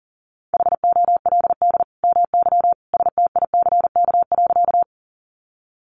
Morse code, heard as a keyed tone, about 40 wpm, 700 Hz. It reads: HOLD MY STICK.
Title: Sentences from Top 1200 Words 40wpm